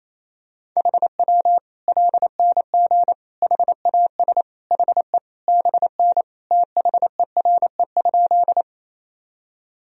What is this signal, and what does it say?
Morse code, heard as a keyed tone, about 28 wpm, 700 Hz. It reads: HW LNG HAS HE BN THERE?
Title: Sentences from Top 100 Words and Abbreviations 28wpm